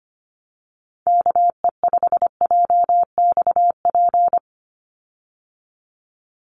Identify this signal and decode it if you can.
Morse code, heard as a keyed tone, about 25 wpm, 700 Hz. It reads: KE5JXP
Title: USA Call Signs 25wpm